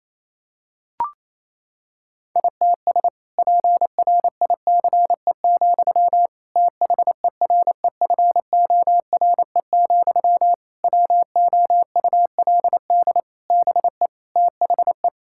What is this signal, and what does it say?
Morse code, heard as a keyed tone, about 28 wpm, 700 Hz. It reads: ITS PRICE, THEREFORE, WOULD BE THE
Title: Sentences from Top 1000 Words - Review Part 5 - 28wpm